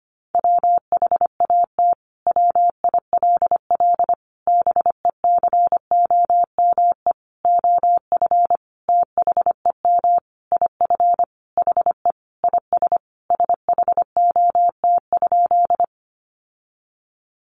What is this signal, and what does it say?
Morse code, heard as a keyed tone, about 25 wpm, 700 Hz. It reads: WHAT WILL BECOME OF THEM IF HE IS SHOT?
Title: Sentences from Top 1100 Words 25wpm